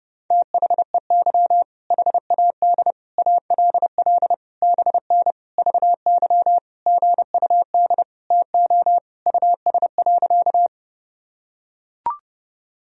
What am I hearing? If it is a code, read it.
Morse code, heard as a keyed tone, about 30 wpm, 700 Hz. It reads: THEY HAD ALL BN VY GUD TO US.
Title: Sentences from Top 200 Words and Abbreviations 30wpm